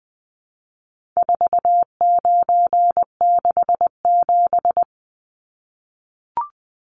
Morse code, heard as a keyed tone, about 20 wpm, 700 Hz. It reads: 4967
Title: Numbers - 4-Digit 20wpm